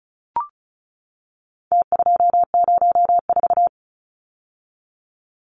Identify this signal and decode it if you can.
Morse code, heard as a keyed tone, about 34 wpm, 700 Hz. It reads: T204
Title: Numbers - Contest Style - 4-Digit 35wpm